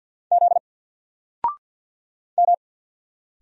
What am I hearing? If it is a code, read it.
Morse code, heard as a keyed tone, about 48 wpm, 700 Hz. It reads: Z M
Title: Single Character (Letter:Number:Punctuation:Prosign) Alt 50wpm